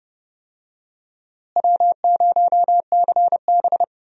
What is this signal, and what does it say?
Morse code, heard as a keyed tone, about 32 wpm, 700 Hz. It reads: W0CB
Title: USA Call Signs - 4 Characters - Repeated 3x - 30wpm